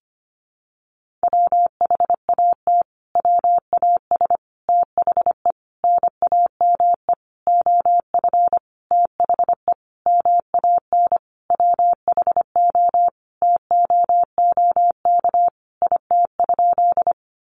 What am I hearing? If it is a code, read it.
Morse code, heard as a keyed tone, about 25 wpm, 700 Hz. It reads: WHAT WAS THE NAME OF THE MAN WHO TOOK IT?
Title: Sentences from Top 300 Words 25wpm